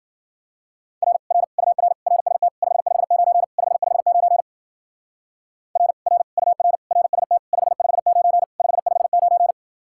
Morse code, heard as a keyed tone, about 48 wpm, 700 Hz. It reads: R R UR RST 559 559 R R UR RST 559 559
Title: Speed-Racing Common QSO Phrases Encore - 40wpm